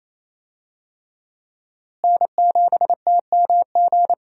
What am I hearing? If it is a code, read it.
Morse code, heard as a keyed tone, about 27 wpm, 700 Hz. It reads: N7TMG
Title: USA Call Signs - 5 Characters - Repeated 3x - 28wpm